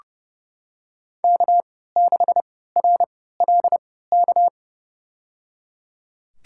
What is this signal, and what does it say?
Morse code, heard as a keyed tone, about 30 wpm, spaced as for 17 wpm, 700 Hz. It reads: K6RLK
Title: USA Call Signs - 5 Characters - 3x Character Spacing - 30wpm